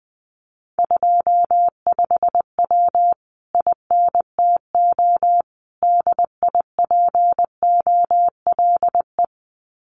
Morse code, heard as a keyed tone, about 20 wpm, 700 Hz. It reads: 25W INTO DIPOLE
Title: Common QSO Phrases Encore 20wpm